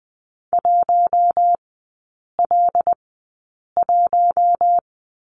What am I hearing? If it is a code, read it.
Morse code, heard as a keyed tone, about 19 wpm, spaced as for 8 wpm, 700 Hz. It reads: 1L1
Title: 074 Lesson 13e - Letter L - Three Characters 20wpm